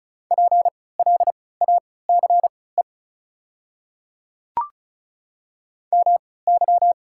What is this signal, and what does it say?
Morse code, heard as a keyed tone, about 36 wpm, spaced as for 20 wpm, 700 Hz. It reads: PLACE MY
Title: 3x Character Spacing - Top 200 Words - 35wpm